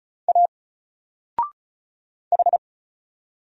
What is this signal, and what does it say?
Morse code, heard as a keyed tone, about 34 wpm, 700 Hz. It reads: A H
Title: Single Letter-Number 35wpm